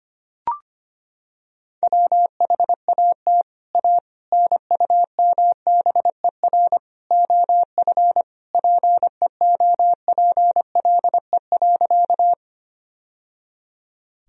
Morse code, heard as a keyed tone, about 25 wpm, 700 Hz. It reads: WHAT A NUMBER OF PEOPLE.
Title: Sentences from Top 1000 Words - Review Part 1 - 25wpm